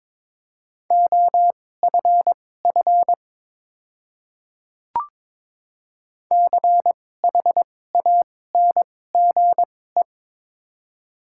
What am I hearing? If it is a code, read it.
Morse code, heard as a keyed tone, about 23 wpm, spaced as for 16 wpm, 700 Hz. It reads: OFF CHANGE
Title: 2x Character Spacing - Top 300 Words - 22wpm